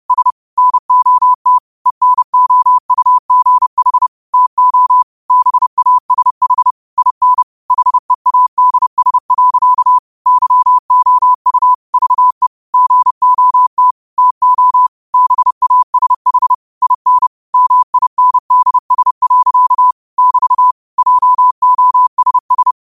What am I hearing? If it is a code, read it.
Morse code, heard as a keyed tone, about 30 wpm, 1000 Hz. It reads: S NOT ENOUGH TO BASH IN HEADS. YOU VE GOT TO BASH IN MINDS. <BT> JOSS
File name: qotdfiles / 30wpm